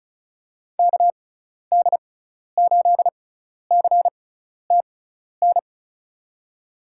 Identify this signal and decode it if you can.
Morse code, heard as a keyed tone, about 34 wpm, spaced as for 12 wpm, 700 Hz. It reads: KD8CTN
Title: USA Call Signs - 6 Characters - 6x Character Spacing - 35wpm